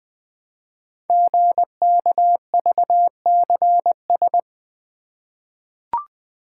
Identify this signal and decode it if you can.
Morse code, heard as a keyed tone, about 20 wpm, 700 Hz. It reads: GKVCS
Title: Code Groups - 5 Letters - 20wpm